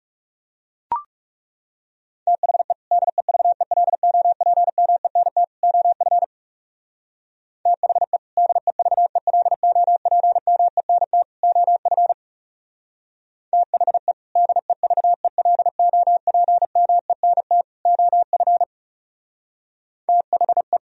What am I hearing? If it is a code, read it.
Morse code, heard as a keyed tone, about 37 wpm, 700 Hz. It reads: THE DEVELOPMENT OF THE DEVELOPMENT OF THE DEVELOPMENT OF THE
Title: Speed-Racing - Sets of 3 Words 30wpm